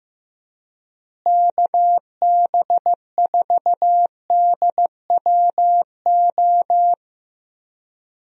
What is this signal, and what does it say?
Morse code, heard as a keyed tone, about 15 wpm, 700 Hz. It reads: KB4DWO
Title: USA Call Signs - 6 Characters - Repeated 2x - 15wpm